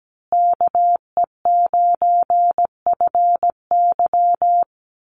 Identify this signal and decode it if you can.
Morse code, heard as a keyed tone, about 17 wpm, 700 Hz. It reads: KE9FY
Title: USA Call Signs Repeated 2x 17wpm